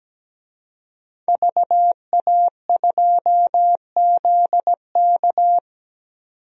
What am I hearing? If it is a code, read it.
Morse code, heard as a keyed tone, about 17 wpm, 700 Hz. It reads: VA2ZK
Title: DXCC-100 Call Signs - 5 Characters - 17wpm